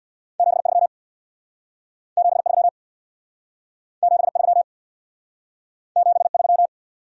Morse code, heard as a keyed tone, about 67 wpm, 700 Hz. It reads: 73 73 73 73